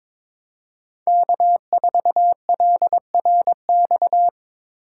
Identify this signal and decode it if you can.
Morse code, heard as a keyed tone, about 22 wpm, 700 Hz. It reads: K4LRX
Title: K1USN SST QSO Practice - 22wpm